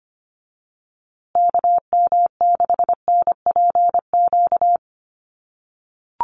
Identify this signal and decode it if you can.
Morse code, heard as a keyed tone, about 25 wpm, 700 Hz. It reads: KM6NPQ
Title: USA Call Signs Repeated 3x 25wpm